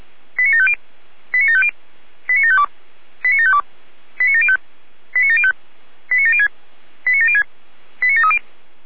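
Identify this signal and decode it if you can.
Five-tone selective call (ZVEI), heard as quick runs of five tones, tones of about 80 ms. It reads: 89750 89750 89752 89752 89705 89705 89706 89706 89730
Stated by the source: Received on an SDR, ZVEI five-tone selective call, code 89533.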